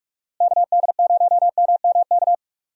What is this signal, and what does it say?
Morse code, heard as a keyed tone, about 42 wpm, 700 Hz. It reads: KD0MMK
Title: USA Call Signs - 6 Characters - Repeated 3x - 45wpm